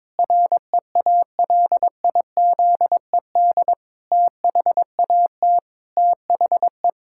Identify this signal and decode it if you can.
Morse code, heard as a keyed tone, about 22 wpm, 700 Hz. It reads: REALIZED THAT THE
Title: Pangram Challenge v2 - 22wpm